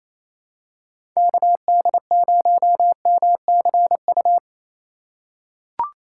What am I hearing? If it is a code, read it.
Morse code, heard as a keyed tone, about 28 wpm, 700 Hz. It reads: KD0MCU
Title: USA Call Signs - 6 Characters - 2x Character Spacing - 28wpm